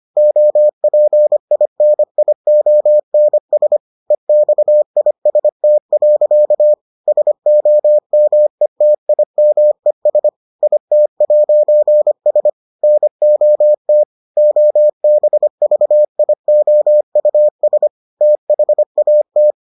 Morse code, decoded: OPINIONS EXIST. SOMETIMES IT'S NOT OBVIOUS THAT

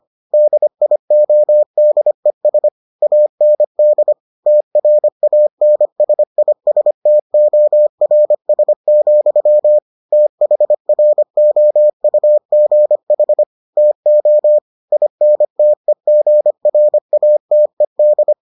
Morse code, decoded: DIODES AND TRANSISTORS, THROUGH TO INTEGRATED

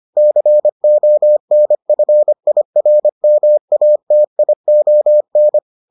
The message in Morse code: CONFIRMATION

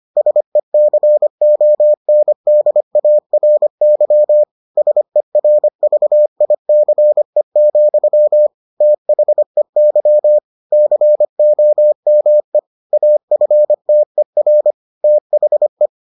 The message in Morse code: SECONDARY SERVICE, THEY COME AFTER THE

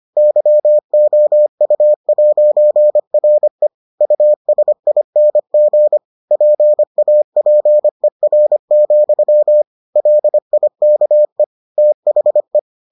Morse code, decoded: YOU'RE USING PAPER, LIKE THE

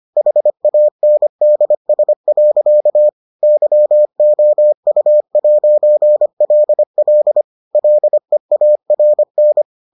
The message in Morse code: HANDS. YOU'LL LEARN

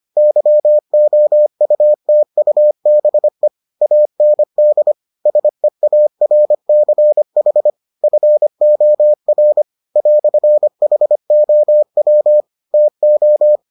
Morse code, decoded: YOUTUBE AND SEARCH FOR "HOW TO

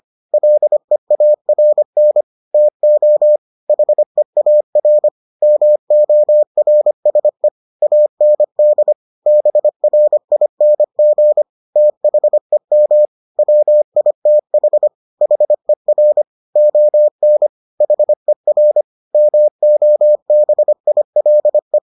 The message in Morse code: LEARN TO HEAR MORSE AND BRING THEM WITH HER ON HER MOBILE

